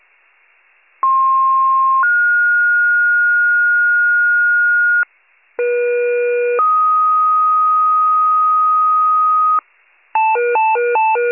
Two-tone page, heard as a one-second tone, then a three-second tone, about 1060 then 1430 Hz, about 480 then 1250 Hz, with noise. It is heard from 1.0 to 9.6 s.